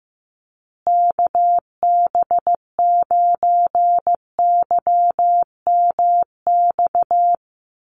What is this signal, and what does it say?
Morse code, heard as a keyed tone, about 14 wpm, 700 Hz. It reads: KB9YMX
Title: USA Call Signs Repeated 3x 15wpm